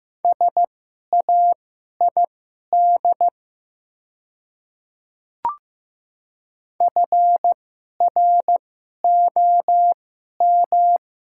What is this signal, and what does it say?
Morse code, heard as a keyed tone, about 15 wpm, spaced as for 11 wpm, 700 Hz. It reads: SAID FROM
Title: 2x Character Spacing - Top 100 Words - 15wpm